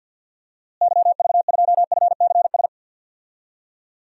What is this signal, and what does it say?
Morse code, heard as a keyed tone, about 53 wpm, 700 Hz. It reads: YV2FCS
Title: DXCC-100 Call Signs - 6 Characters - Repeated 2x - 50wpm